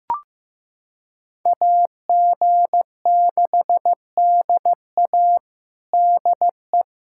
Morse code, heard as a keyed tone, about 15 wpm, 700 Hz. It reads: AG6DA DE
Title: USA Callsign DE Callsign - 5 Characters - 15wpm